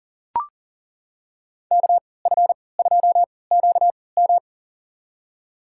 Morse code, heard as a keyed tone, about 40 wpm, spaced as for 23 wpm, 700 Hz. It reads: KF2QM
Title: USA Call Signs - 5 Characters - 3x Character Spacing - 40wpm